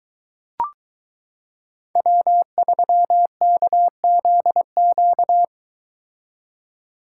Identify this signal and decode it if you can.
Morse code, heard as a keyed tone, about 22 wpm, 700 Hz. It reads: W3KZQ